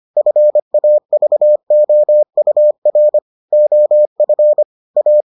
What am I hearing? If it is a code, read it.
Morse code, heard as a keyed tone, about 25 wpm, 600 Hz. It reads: FAVOUR OF A